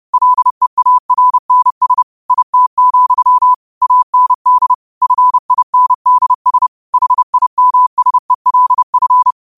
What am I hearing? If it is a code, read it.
Morse code, heard as a keyed tone, about 30 wpm, 1000 Hz. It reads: LEARNS IT, AND FINDS HIMSELF